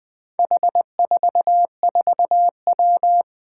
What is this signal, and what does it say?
Morse code, heard as a keyed tone, about 20 wpm, 700 Hz. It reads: H44W
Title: Four Challenging Letter-Number Combinations 20wpm